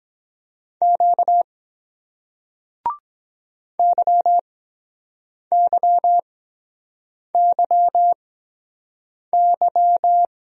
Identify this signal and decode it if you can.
Morse code, heard as a keyed tone, about 23 wpm, spaced as for 7 wpm, 700 Hz. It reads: Q YYYY